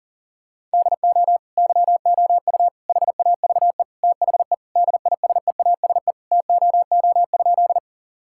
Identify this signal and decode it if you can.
Morse code, heard as a keyed tone, about 40 wpm, 700 Hz. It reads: DO YOU HAVE THE DISEASE TOO?
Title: Sentences from Top 1000 Words - Review Part 6 - 40wpm